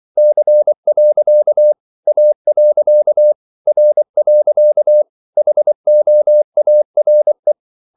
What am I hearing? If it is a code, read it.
Morse code, heard as a keyed tone, about 24 wpm, 600 Hz. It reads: C. A. R. HOARE
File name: CW / Practice / EnglishQuotations0224